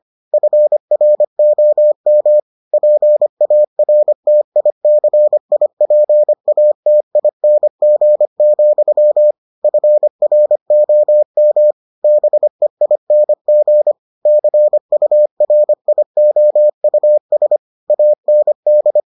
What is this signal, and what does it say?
Morse code, heard as a keyed tone, about 25 wpm, 600 Hz. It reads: FROM PARTICIPATING, FROM BEING CURIOUS AND